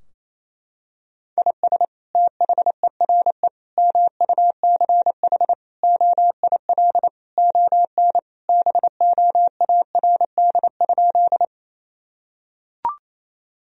Morse code, heard as a keyed tone, about 28 wpm, 700 Hz. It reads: IS THERE MUCH OIL ON BOARD?
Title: Sentences from Top 1000 Words - Review Part 4 - 28wpm